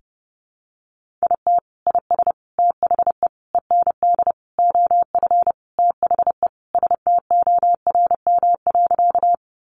Morse code, heard as a keyed tone, about 30 wpm, 700 Hz. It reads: IT IS THE END OF THE STORM.
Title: Sentences from Top 1500 Words 30wpm